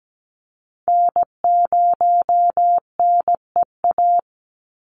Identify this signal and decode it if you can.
Morse code, heard as a keyed tone, about 17 wpm, 700 Hz. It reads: N0NEA
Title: USA Call Signs 17wpm